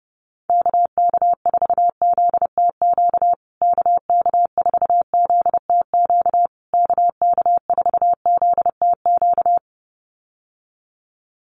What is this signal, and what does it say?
Morse code, heard as a keyed tone, about 28 wpm, 700 Hz. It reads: KK4ZTQ KK4ZTQ KK4ZTQ
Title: USA Call Signs Repeated 3x 30wpm